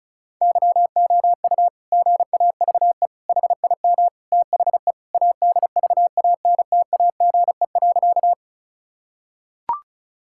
Morse code, heard as a keyed tone, about 36 wpm, 690 Hz. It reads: YOU GAVE HIM THE ADVANTAGE.
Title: Sentences from Top 1000 Words - Review Part 6 - 35wpm